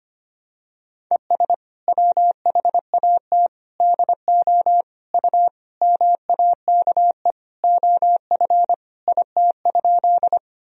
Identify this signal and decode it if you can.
Morse code, heard as a keyed tone, about 25 wpm, 700 Hz. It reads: ES WHAT DO U MAKE OF IT?
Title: Sentences from Top 100 Words and Abbreviations 25wpm